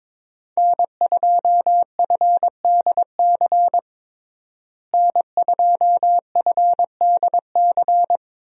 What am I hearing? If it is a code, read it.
Morse code, heard as a keyed tone, about 22 wpm, 700 Hz. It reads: N2FDC N2FDC
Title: USA Call Signs - 5 Characters - Repeated 2x - 22wpm